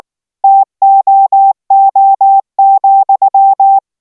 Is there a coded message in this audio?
Yes: Morse code from 0.4 to 3.8 s, heard as a keyed tone, about 19 wpm, 780 Hz. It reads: TOO,